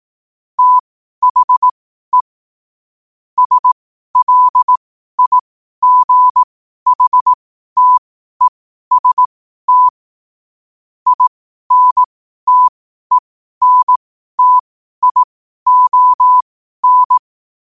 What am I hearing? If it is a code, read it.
Morse code, heard as a keyed tone, about 18 wpm, spaced as for 12 wpm, 1000 Hz. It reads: THE SLIGHTEST INTENTION
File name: qotdfiles / 12wpm